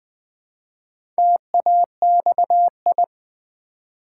Morse code, heard as a keyed tone, about 20 wpm, 700 Hz. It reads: TAXI
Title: Build A Word - 1 Letter At A Time - 4-Letter Words - 20wpm